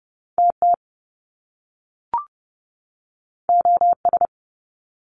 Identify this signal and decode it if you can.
Morse code, heard as a keyed tone, about 30 wpm, 700 Hz. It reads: TT OS
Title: Top Two Letter Combinations in Words 30wpm